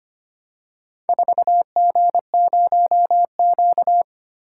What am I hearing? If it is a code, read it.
Morse code, heard as a keyed tone, about 25 wpm, 700 Hz. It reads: 4G0Q
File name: practice / 25wpm / Four Letter-Number Combinations - 25wpm